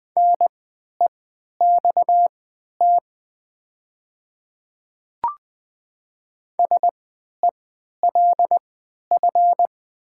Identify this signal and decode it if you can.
Morse code, heard as a keyed tone, about 20 wpm, spaced as for 11 wpm, 700 Hz. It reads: NEXT SELF
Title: 3x Character Spacing - Top 300 Words - 20wpm